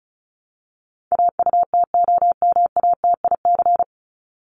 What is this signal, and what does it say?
Morse code, heard as a keyed tone, about 34 wpm, 700 Hz. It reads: AUTOMATIC
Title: Top 1751-2000 Words 35wpm